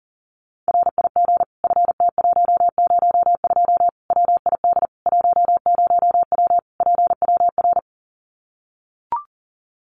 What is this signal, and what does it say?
Morse code, heard as a keyed tone, about 40 wpm, 700 Hz. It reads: RIG FT102 WID 10W PWR
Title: Common QSO Phrases 40wpm